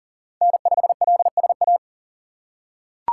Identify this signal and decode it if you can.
Morse code, heard as a keyed tone, about 38 wpm, 700 Hz. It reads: N5LSA
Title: USA Call Signs - 5 Characters - 2x Character Spacing - 40wpm